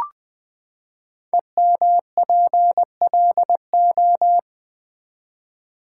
Morse code, heard as a keyed tone, about 20 wpm, 700 Hz. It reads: EMPLO
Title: Build A Word - 5 Letters At A Time - 10-Letter Words - 20wpm